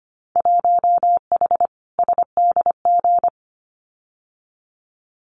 Morse code, heard as a keyed tone, about 25 wpm, 700 Hz. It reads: 1H SDG
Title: Winter Field Day Exchanges 25wpm